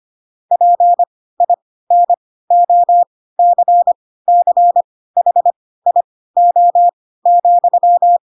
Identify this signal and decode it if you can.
Morse code, heard as a keyed tone, about 25 wpm, spaced as for 16 wpm, 700 Hz. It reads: PINOCCHIO,